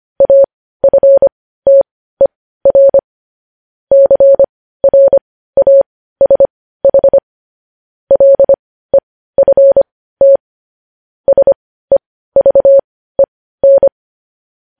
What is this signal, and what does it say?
Morse code, heard as a keyed tone, about 25 wpm, spaced as for 15 wpm, 550 Hz. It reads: AFTER CRASH LEFT SEVEN
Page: news headlines in morse code at 15 wpm